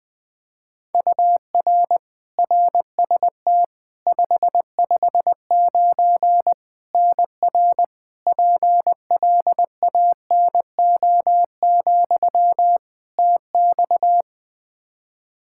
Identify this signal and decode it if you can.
Morse code, heard as a keyed tone, about 20 wpm, 700 Hz. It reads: UR RST 559 NR PLANO, TX
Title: Practice QSO - Exchange 1 - Beginning - 20wpm